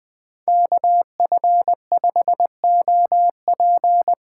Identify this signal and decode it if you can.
Morse code, heard as a keyed tone, about 20 wpm, 700 Hz. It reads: KF5OP
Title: Speed Racing - USA 2x2 Call Signs 17wpm